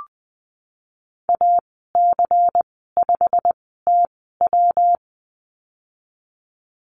Morse code, heard as a keyed tone, about 20 wpm, spaced as for 14 wpm, 700 Hz. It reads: AC5TW
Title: 128 Lesson 21g - Number 6 - Call Signs 20wpm